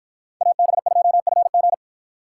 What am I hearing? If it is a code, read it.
Morse code, heard as a keyed tone, about 53 wpm, 700 Hz. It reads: AB2FG